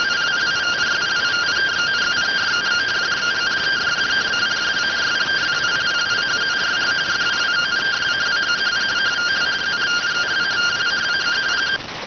Two-tone teleprinter signal, about 170 Hz shift.